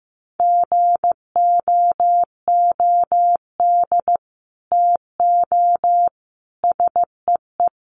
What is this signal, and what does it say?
Morse code, heard as a keyed tone, about 15 wpm, 700 Hz. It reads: GOOD TO SEE
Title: Sets of 4 Words 15wpm